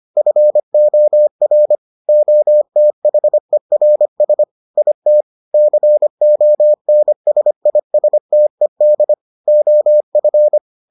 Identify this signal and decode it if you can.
Morse code, heard as a keyed tone, about 25 wpm, 600 Hz. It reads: FOR OTHERS IT CONSISTED OF